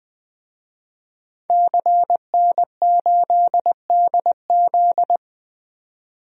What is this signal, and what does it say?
Morse code, heard as a keyed tone, about 20 wpm, 700 Hz. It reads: CN8DZ
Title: DXCC-100 Call Signs - 5 Characters - 20wpm